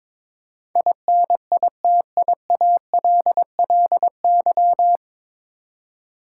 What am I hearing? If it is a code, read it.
Morse code, heard as a keyed tone, about 22 wpm, 700 Hz. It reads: INITIALLY
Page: Morse Code Ninja practice 22wpm